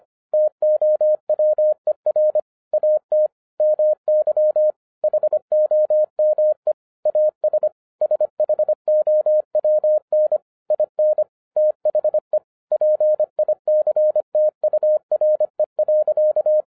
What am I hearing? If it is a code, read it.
Morse code, heard as a keyed tone, about 25 wpm, 610 Hz. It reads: TOWER AT MY HOME AS SHOWN IN THE PICTURE.